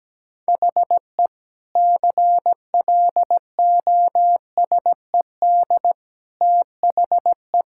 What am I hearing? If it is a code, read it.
Morse code, heard as a keyed tone, about 17 wpm, 700 Hz. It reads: HE CLOSED THE
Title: Sentences from Top 1600 Words 17wpm